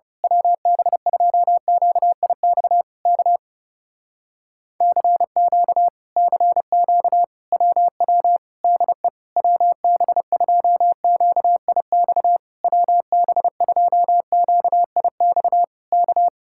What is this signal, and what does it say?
Morse code, heard as a keyed tone, about 28 wpm, 700 Hz. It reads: WB2QIX K CQ CQ WW DE WB2QIX WB2QIX K
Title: Speed-Racing Common QSO Phrases Encore - 30wpm